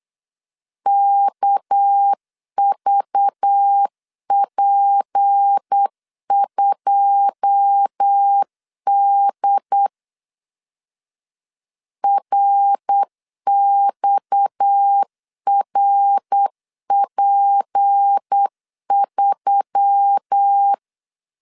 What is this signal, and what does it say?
Morse code, heard as a keyed tone, about 8 wpm, 800 Hz. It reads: KVP2D RXRP3
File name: LEZIONI / 9 2 8wpm 3437